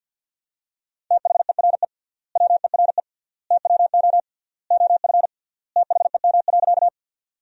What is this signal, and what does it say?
Morse code, heard as a keyed tone, about 52 wpm, 700 Hz. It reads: THERE WERE TWO OF THEM.
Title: 3x Word-Spacing - Sentences from Top 100 Words - 50wpm